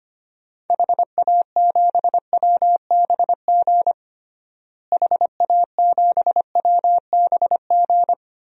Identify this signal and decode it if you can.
Morse code, heard as a keyed tone, about 25 wpm, 700 Hz. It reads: HA7WBG HA7WBG
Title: DXCC-100 Call Signs - 6 Characters - Repeated 2x - 25wpm